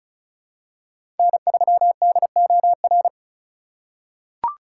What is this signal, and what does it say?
Morse code, heard as a keyed tone, about 36 wpm, 700 Hz. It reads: N3DOR
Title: USA Call Signs - 5 Characters - 2x Character Spacing - 35wpm